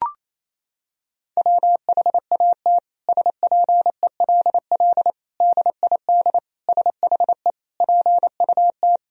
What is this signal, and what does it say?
Morse code, heard as a keyed tone, about 28 wpm, 700 Hz. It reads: WHAT SPELL DID SHE PUT
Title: Sentences from Top 1000 Words - Review Part 1 - 28wpm